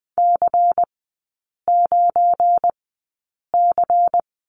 Morse code, heard as a keyed tone, about 20 wpm, spaced as for 8 wpm, 690 Hz. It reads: C9C
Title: 198 Lesson 32e - Letter k - Three Characters 20wpm